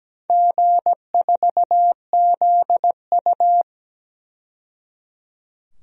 Morse code, heard as a keyed tone, about 17 wpm, 700 Hz. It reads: G4ZU
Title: DXCC-100 Call Signs - 4 Characters - 17wpm